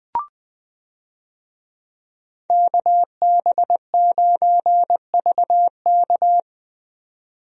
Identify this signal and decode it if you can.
Morse code, heard as a keyed tone, about 20 wpm, 700 Hz. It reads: KB9VK
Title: USA Call Signs - 5 Characters - 20wpm